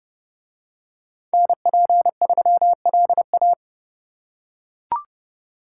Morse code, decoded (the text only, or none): NP3LA